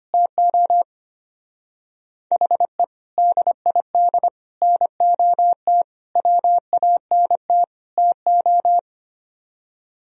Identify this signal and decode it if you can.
Morse code, heard as a keyed tone, about 31 wpm, 700 Hz. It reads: TO HE DID NOT WANT TO
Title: Speed-Racing - Sets of 5 Words 25wpm